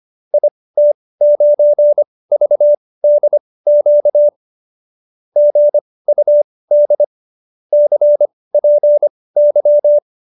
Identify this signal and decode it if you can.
Morse code, heard as a keyed tone, about 25 wpm, spaced as for 18 wpm, 590 Hz. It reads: IT9VDQ GUD CPY
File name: qso215b 18